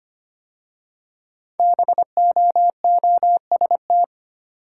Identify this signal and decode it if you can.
Morse code, heard as a keyed tone, about 25 wpm, 700 Hz. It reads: BOOST
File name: 5 Letter Words - First 3 Letters Given - 25wpm